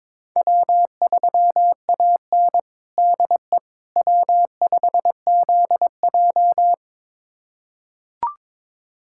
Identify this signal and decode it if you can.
Morse code, heard as a keyed tone, about 22 wpm, 700 Hz. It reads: W3AN DE W5ZJ
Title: USA Callsign DE Callsign - 4 Characters - 22wpm